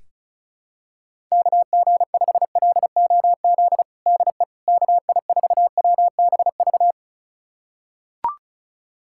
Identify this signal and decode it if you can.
Morse code, heard as a keyed tone, about 34 wpm, 700 Hz. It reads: KG5LOZ DE KI4WBV
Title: USA Callsign DE Callsign - 6 Characters - 35wpm